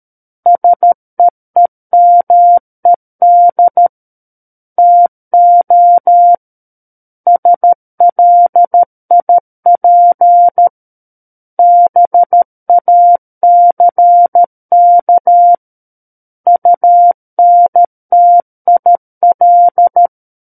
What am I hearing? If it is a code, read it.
Morse code, heard as a keyed tone, about 13 wpm, 700 Hz. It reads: SEEMED TO SLIP BACK UNTIL
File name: Book One / 04-Chapter 4 - 13 WPM